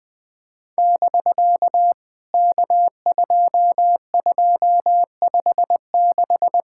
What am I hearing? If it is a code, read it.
Morse code, heard as a keyed tone, about 19 wpm, 700 Hz. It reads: <BK> K2256
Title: POTA QSO Practice - 20wpm